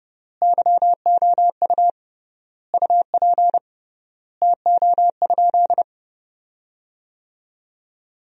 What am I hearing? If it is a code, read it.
Morse code, heard as a keyed tone, about 31 wpm, 700 Hz. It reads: YOU UP TO?
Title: 3x Word-Spacing - Sentences from Top 100 Words - 30wpm